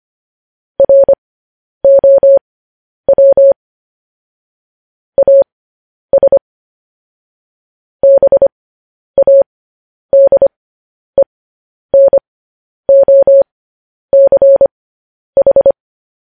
Morse code, heard as a keyed tone, about 25 wpm, spaced as for 10 wpm, 550 Hz. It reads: ROW AS BADENOCH
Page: news headlines in morse code at 10 wpm